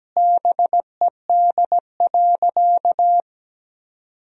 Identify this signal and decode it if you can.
Morse code, heard as a keyed tone, about 17 wpm, 700 Hz. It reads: BED.
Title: 4x Word-Spacing - Sentences from Top 500 Words - 17wpm